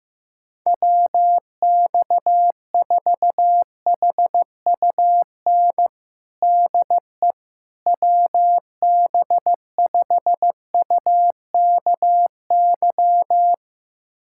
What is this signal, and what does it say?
Morse code, heard as a keyed tone, about 15 wpm, 700 Hz. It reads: WX4HUN DE WB5UKY
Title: USA Callsign DE Callsign - 6 Characters - 15wpm